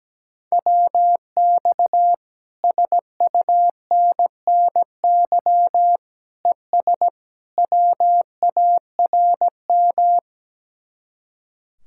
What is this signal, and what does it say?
Morse code, heard as a keyed tone, about 17 wpm, 700 Hz. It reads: WX SUNNY ES WARM
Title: Speed-Racing Common QSO Phrases Encore - 17wpm